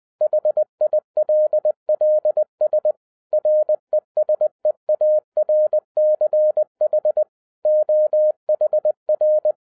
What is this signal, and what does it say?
Morse code, heard as a keyed tone, about 20 wpm, 610 Hz. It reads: HILLS RESEARCH OHR
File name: 2011-07-mp3 / n23 QRP OTAKU 20wpm